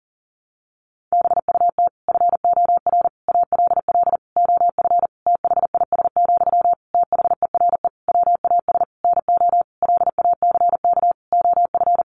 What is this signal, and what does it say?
Morse code, heard as a keyed tone, about 37 wpm, 700 Hz. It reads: BUT FOR ALL OF THIS, THERE WAS NO LACK OF
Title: Sentences from Top 1300 Words 40wpm